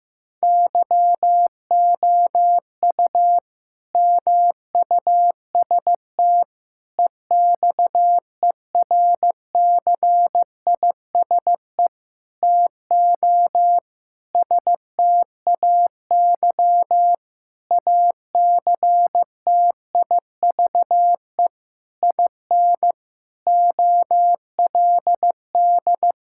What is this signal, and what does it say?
Morse code, heard as a keyed tone, about 16 wpm, 700 Hz. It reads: YOU MUST EXERCISE TO STAY ACTIVE IN OLD
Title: Sentences from Top 1000 Words - Review Part 6 - 15wpm